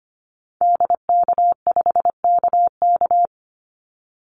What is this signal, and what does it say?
Morse code, heard as a keyed tone, about 25 wpm, 700 Hz. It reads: DK5KK
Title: Call Signs Repeated 2x - 25wpm